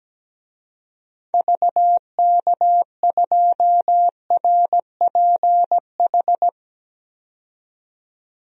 Morse code, heard as a keyed tone, about 17 wpm, 700 Hz. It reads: VK2RPH
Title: DXCC-100 Call Signs - 6 Characters - 17wpm